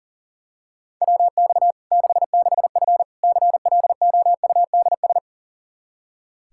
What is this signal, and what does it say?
Morse code, heard as a keyed tone, about 40 wpm, 700 Hz. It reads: WX 66F CLOUDS
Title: Practice QSO - Exchange 2 - Info - 40wpm